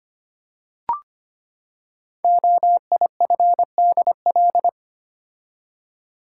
Morse code, heard as a keyed tone, about 25 wpm, 700 Hz. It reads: OIFDL
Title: Code Groups - 5 Letters - 25wpm